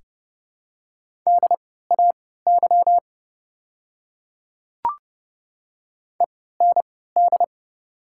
Morse code, heard as a keyed tone, about 30 wpm, spaced as for 17 wpm, 700 Hz. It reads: DAY END
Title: 3x Character Spacing - Top 300 Words - 30wpm